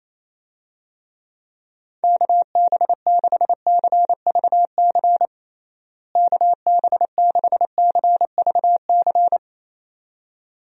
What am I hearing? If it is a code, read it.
Morse code, heard as a keyed tone, about 27 wpm, 700 Hz. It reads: KB6CVC KB6CVC
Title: USA Call Signs - 6 Characters - Repeated 2x - 28wpm